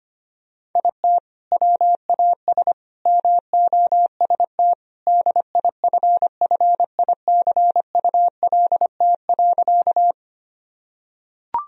Morse code, heard as a keyed tone, about 25 wpm, 700 Hz. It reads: IT WAS MOST DIFFICULT.
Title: Sentences from Top 1000 Words - Review Part 1 - 25wpm